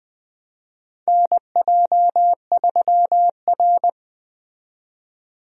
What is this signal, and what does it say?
Morse code, heard as a keyed tone, about 20 wpm, 700 Hz. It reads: NJ3R
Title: USA Call Signs - 4 Characters - 20wpm